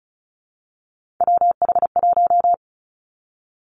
Morse code, heard as a keyed tone, about 36 wpm, 700 Hz. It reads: WH1
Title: Three Challenging Letter-Number Combinations 35wpm